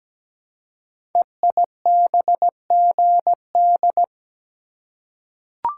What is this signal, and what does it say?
Morse code, heard as a keyed tone, about 17 wpm, 700 Hz. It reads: EIBGD
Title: Code Groups - 5 Letters - 17wpm